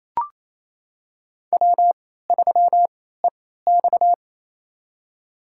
Morse code, heard as a keyed tone, about 28 wpm, spaced as for 16 wpm, 700 Hz. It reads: W3EX
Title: USA Call Signs - 4 Characters - 3x Character Spacing - 28wpm